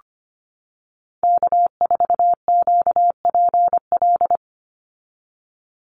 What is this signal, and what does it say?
Morse code, heard as a keyed tone, about 25 wpm, 700 Hz. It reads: K4QPL
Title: Call Signs - 25wpm